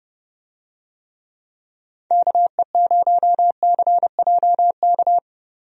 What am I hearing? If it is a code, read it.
Morse code, heard as a keyed tone, about 30 wpm, 700 Hz. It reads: KE0CJK
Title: USA Call Signs - 6 Characters - Repeated 2x - 30wpm